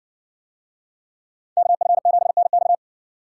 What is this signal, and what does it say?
Morse code, heard as a keyed tone, about 67 wpm, 700 Hz. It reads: KF7NX